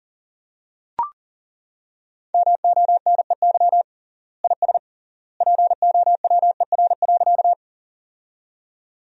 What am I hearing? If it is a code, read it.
Morse code, heard as a keyed tone, about 37 wpm, 700 Hz. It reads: MONEY IS POWER.
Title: 3x Word-Spacing - Sentences from Top 500 Words - 40wpm